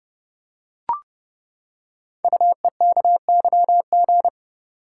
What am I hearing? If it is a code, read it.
Morse code, heard as a keyed tone, about 28 wpm, 700 Hz. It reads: UEKYG
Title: Code Groups - 5 Letters - 30wpm